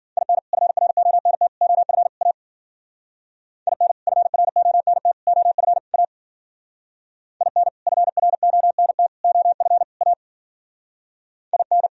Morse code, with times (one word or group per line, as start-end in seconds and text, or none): 0.2-0.4 IN
0.5-1.5 FRONT
1.6-2.1 OF
2.2-2.3 A
3.7-3.9 IN
4.1-5.1 FRONT
5.3-5.8 OF
5.9-6.0 A
7.4-7.7 IN
7.9-9.1 FRONT
9.2-9.8 OF
10.0-10.1 A
11.5-11.9 IN